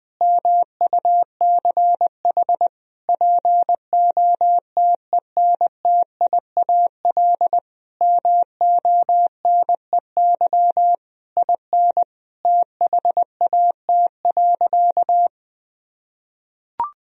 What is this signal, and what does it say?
Morse code, heard as a keyed tone, about 20 wpm, 700 Hz. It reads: MUCH POTENTIAL MONEY IN THAT.
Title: Sentences from Top 1000 Words - Review Part 4 - 20wpm